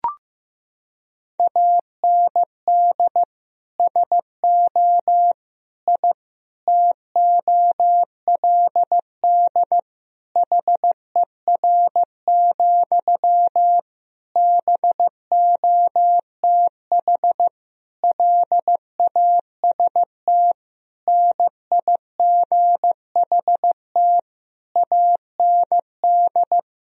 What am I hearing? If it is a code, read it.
Morse code, heard as a keyed tone, about 15 wpm, 700 Hz. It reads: AND SO I TOLD HER, BOTH LAST NIGHT AND
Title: Sentences from Top 1000 Words - Review Part 4 - 15wpm